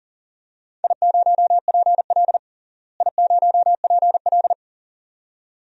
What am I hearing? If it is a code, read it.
Morse code, heard as a keyed tone, about 38 wpm, 700 Hz. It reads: I0PL I0PL
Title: DXCC-100 Call Signs - 4 Characters - Repeated 2x - 40wpm